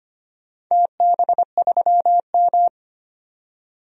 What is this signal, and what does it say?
Morse code, heard as a keyed tone, about 25 wpm, 700 Hz. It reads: TB3M
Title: DXCC-100 Call Signs - 4 Characters - 25wpm